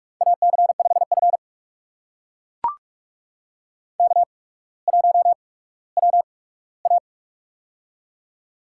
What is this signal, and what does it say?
Morse code, heard as a keyed tone, about 42 wpm, spaced as for 19 wpm, 700 Hz. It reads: AC5F K1WA
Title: USA Call Signs - 4 Characters - 8x Character Spacing - 45wpm